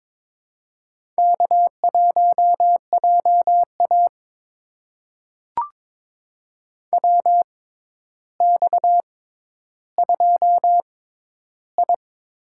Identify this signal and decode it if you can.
Morse code, heard as a keyed tone, about 22 wpm, spaced as for 11 wpm, 700 Hz. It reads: K1JA WX2I
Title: USA Call Signs - 4 Characters - 6x Character Spacing - 22wpm